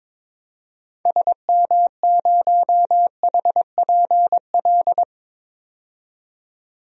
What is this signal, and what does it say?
Morse code, heard as a keyed tone, about 22 wpm, 700 Hz. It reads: SM0HPL
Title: DXCC-100 Call Signs - 6 Characters - 22wpm